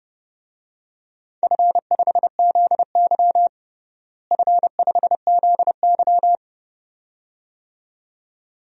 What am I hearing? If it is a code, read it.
Morse code, heard as a keyed tone, about 32 wpm, 700 Hz. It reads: F5ZY F5ZY
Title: DXCC-100 Call Signs - 4 Characters - Repeated 2x - 30wpm